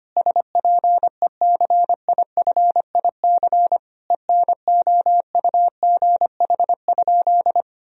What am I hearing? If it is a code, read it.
Morse code, heard as a keyed tone, about 25 wpm, 700 Hz. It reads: SPECIFIC ENOUGH?
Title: Sentences from Top 1000 Words - Review Part 3 - 25wpm